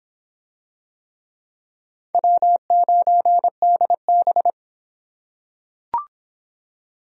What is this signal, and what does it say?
Morse code, heard as a keyed tone, about 26 wpm, 700 Hz. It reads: W9DB